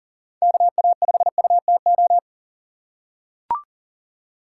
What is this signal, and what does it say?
Morse code, heard as a keyed tone, about 40 wpm, 700 Hz. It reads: KA5UTO
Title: USA Call Signs - 6 Characters - 4x Character Spacing - 40wpm